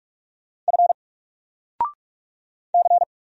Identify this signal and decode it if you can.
Morse code, heard as a keyed tone, about 44 wpm, 700 Hz. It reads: F C
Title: Single Character (Letter:Number:Punctuation:Prosign) Alt 45wpm